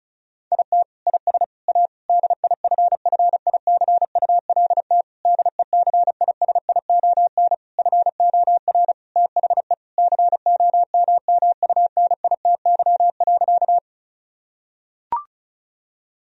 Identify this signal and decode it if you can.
Morse code, heard as a keyed tone, about 34 wpm, 700 Hz. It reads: IT IS A DIFFICULT DECISION FOR THE COMMUNITY.
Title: Sentences from Top 1000 Words - Review Part 4 - 35wpm